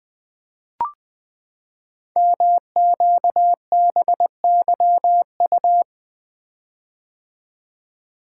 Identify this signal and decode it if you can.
Morse code, heard as a keyed tone, about 20 wpm, 700 Hz. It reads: MQBYU
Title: Code Groups - 5 Letters - 20wpm